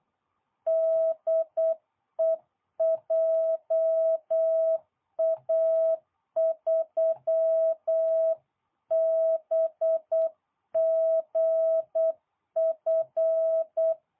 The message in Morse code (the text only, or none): DE JA3BGF